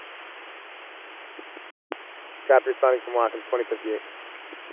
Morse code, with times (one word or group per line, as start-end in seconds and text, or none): none